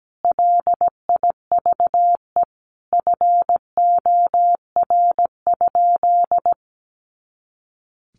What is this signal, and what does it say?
Morse code, heard as a keyed tone, about 17 wpm, 700 Hz. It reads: LIVE FOR?
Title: Sentences from Top 200 Words 17wpm